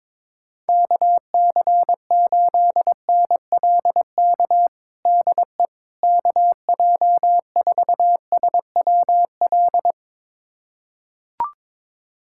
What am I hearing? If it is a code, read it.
Morse code, heard as a keyed tone, about 22 wpm, 700 Hz. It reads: KC8NLK DE KJ4SWL
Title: USA Callsign DE Callsign 22wpm